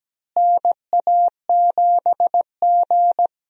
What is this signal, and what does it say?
Morse code, heard as a keyed tone, about 17 wpm, 700 Hz. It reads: NA7G
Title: USA Call Signs - 4 Characters - Repeated 3x - 17wpm